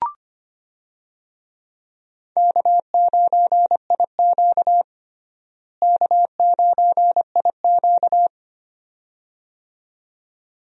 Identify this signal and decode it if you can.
Morse code, heard as a keyed tone, about 25 wpm, 700 Hz. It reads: K9IQ K9IQ
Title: USA Call Signs - 4 Characters - Repeated 2x - 25wpm